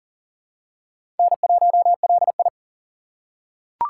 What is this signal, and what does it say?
Morse code, heard as a keyed tone, about 37 wpm, 700 Hz. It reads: N1LI
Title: USA Call Signs - 4x Character Spacing - 40wpm